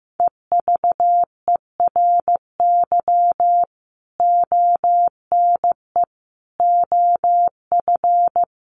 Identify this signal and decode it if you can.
Morse code, heard as a keyed tone, about 15 wpm, 700 Hz. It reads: EVERY ONE OF
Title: Sentences from Top 1500 Words 15wpm